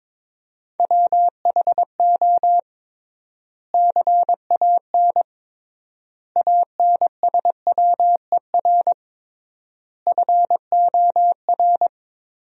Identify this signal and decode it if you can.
Morse code, heard as a keyed tone, about 21 wpm, 700 Hz. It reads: WHO CAN ANSWER FOR
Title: 3x Word-Spacing - Sentences from Top 300 Words - 22wpm